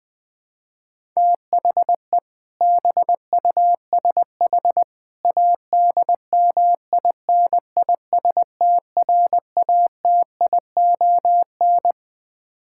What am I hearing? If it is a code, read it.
Morse code, heard as a keyed tone, about 21 wpm, 700 Hz. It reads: THE BUSH ADMINISTRATION
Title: Build A Phrase - 3-Word Phrases - 20wpm